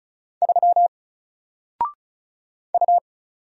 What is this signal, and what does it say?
Morse code, heard as a keyed tone, about 36 wpm, 700 Hz. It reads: 3 U
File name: Single-Letter-Number-35wpm